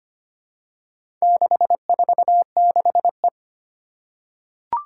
Morse code, heard as a keyed tone, about 25 wpm, 700 Hz. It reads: 646E
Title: Four Challenging Letter-Number Combinations 25wpm